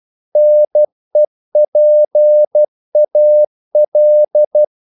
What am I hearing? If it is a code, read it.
Morse code, heard as a keyed tone, about 12 wpm, 600 Hz. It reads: NEPAL
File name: CW / Practice / NationalCapitals12